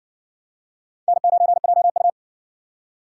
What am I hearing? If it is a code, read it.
Morse code, heard as a keyed tone, about 52 wpm, 700 Hz. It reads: N9JU